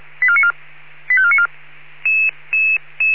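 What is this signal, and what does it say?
Five-tone selective call (ZVEI), heard as quick runs of five tones, tones of about 70 ms. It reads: 85494
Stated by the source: ZVEI five-tone call 85492 received by SDR.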